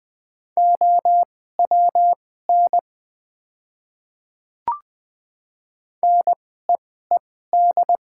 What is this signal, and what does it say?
Morse code, heard as a keyed tone, about 20 wpm, spaced as for 14 wpm, 700 Hz. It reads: OWN NEED
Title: 2x Character Spacing - Top 300 Words - 20wpm